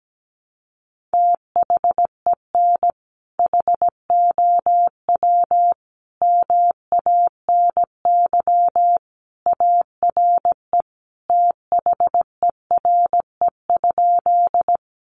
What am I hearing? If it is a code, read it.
Morse code, heard as a keyed tone, about 17 wpm, 700 Hz. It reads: THEN HOW MANY ARE THERE?
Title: Sentences from Top 100 Words 17wpm